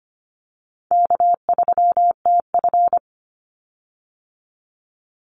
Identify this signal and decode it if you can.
Morse code, heard as a keyed tone, about 25 wpm, 700 Hz. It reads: K3TF
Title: Call Signs Repeated 2x - 25wpm